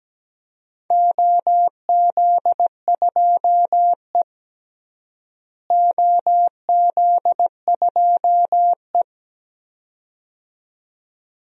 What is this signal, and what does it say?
Morse code, heard as a keyed tone, about 17 wpm, 700 Hz. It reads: OZ2E OZ2E
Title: DXCC-100 Call Signs - 4 Characters - Repeated 2x - 17wpm